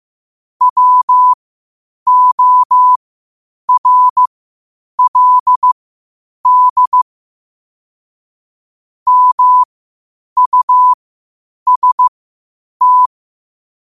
Morse code, heard as a keyed tone, about 15 wpm, spaced as for 8 wpm, 1000 Hz. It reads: WORLD MUST
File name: qotdfiles / 8wpm